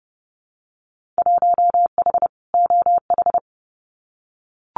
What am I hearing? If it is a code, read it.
Morse code, heard as a keyed tone, about 32 wpm, 700 Hz. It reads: 1H OH
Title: Winter Field Day Exchanges 30wpm